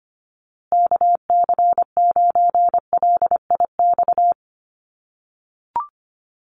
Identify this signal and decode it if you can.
Morse code, heard as a keyed tone, about 25 wpm, 700 Hz. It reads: KC9LIX
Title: USA Call Signs Repeated 3x 25wpm